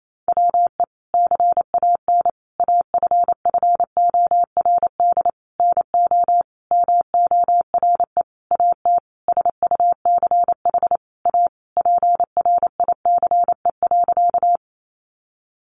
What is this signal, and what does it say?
Morse code, heard as a keyed tone, about 28 wpm, 700 Hz. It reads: WE CAN AFFORD NO MORE AT SUCH A PRICE.
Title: Sentences from Top 1100 Words 28wpm